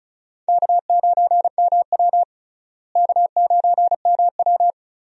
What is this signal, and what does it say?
Morse code, heard as a keyed tone, about 36 wpm, 700 Hz. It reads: K9MW K9MW
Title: USA Call Signs - 4 Characters - Repeated 2x - 35wpm